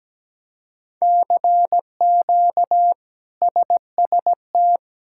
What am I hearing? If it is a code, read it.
Morse code, heard as a keyed tone, about 17 wpm, 700 Hz. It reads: CQ SST
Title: K1USN SST QSO Practice - 17wpm